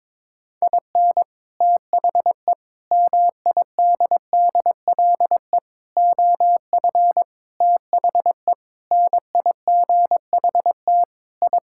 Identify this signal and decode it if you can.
Morse code, heard as a keyed tone, about 22 wpm, 700 Hz. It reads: IN THE MIDDLE OF THE NIGHT I
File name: practice / 22wpm / Sentences with 4-Word Phrases - 22wpm